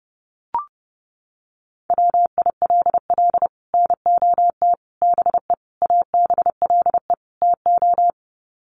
Morse code, heard as a keyed tone, about 30 wpm, 700 Hz. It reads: WILL NOT BE ABLE TO
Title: Sets of 5 Words 30wpm